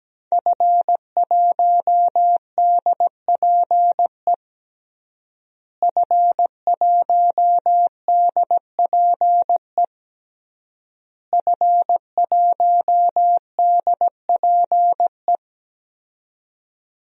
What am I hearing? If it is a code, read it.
Morse code, heard as a keyed tone, about 17 wpm, 700 Hz. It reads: F1DPE F1DPE F1DPE
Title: DXCC-100 Call Signs - 5 Characters - Repeated 3x - 17wpm